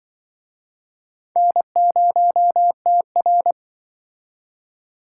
Morse code, heard as a keyed tone, about 26 wpm, 700 Hz. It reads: N0TR